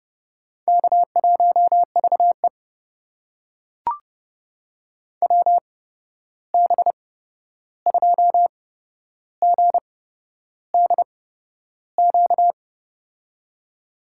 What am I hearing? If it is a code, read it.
Morse code, heard as a keyed tone, about 30 wpm, spaced as for 11 wpm, 700 Hz. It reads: K1VE WB2GDQ
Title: USA Call Signs - 8x Character Spacing - 30wpm